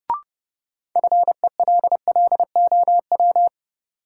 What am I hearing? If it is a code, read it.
Morse code, heard as a keyed tone, about 30 wpm, 700 Hz. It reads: FELLOW
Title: Sets of 5 Words Encore - New Word Drill - 30wpm